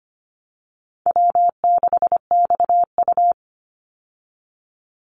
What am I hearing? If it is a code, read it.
Morse code, heard as a keyed tone, about 25 wpm, 700 Hz. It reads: W6XU
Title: Easy Call Signs Repeated 2x - 25wpm